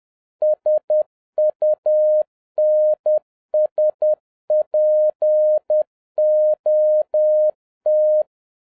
Morse code, heard as a keyed tone, about 10 wpm, 610 Hz. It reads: SUNSPOT